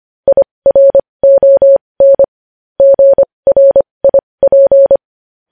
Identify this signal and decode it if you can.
Morse code, heard as a keyed tone, about 25 wpm, spaced as for 20 wpm, 550 Hz. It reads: IRON GRIP
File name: podcast / News headlines-1x-25x20